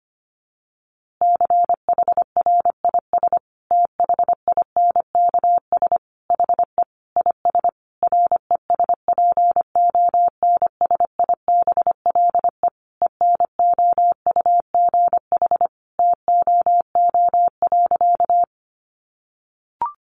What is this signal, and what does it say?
Morse code, heard as a keyed tone, about 25 wpm, 700 Hz. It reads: CHRIS THINKS HE IS RESPONSIBLE ENOUGH TOO.
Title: Easy Phrases 25wpm